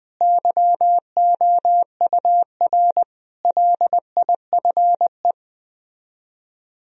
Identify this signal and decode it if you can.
Morse code, heard as a keyed tone, about 20 wpm, 700 Hz. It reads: YOUR LIFE
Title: Speed-Racing - Sets of 5 Words 17wpm